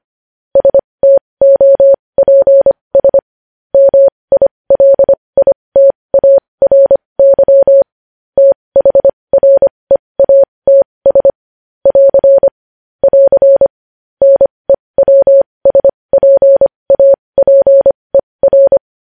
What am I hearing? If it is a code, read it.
Morse code, heard as a keyed tone, about 25 wpm, spaced as for 20 wpm, 550 Hz. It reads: STOPS MILITARY THREATS <AR> <AR> NEWSPAPER
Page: news headlines in morse code at 20 wpm